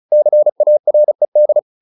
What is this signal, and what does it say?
Morse code, heard as a keyed tone, about 34 wpm, 600 Hz. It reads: CARED